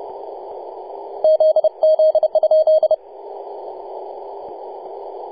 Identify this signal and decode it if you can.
Morse code, heard as a keyed tone, about 30 wpm, 630 Hz. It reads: ZZ?
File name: posts / 2015-nov-18-0552z-vk9wa-24-894mhz-cw